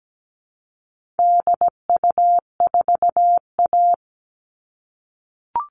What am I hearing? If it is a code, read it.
Morse code, heard as a keyed tone, about 17 wpm, 700 Hz. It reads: DU4A
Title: All Cut Numbers - 4-Digit 17wpm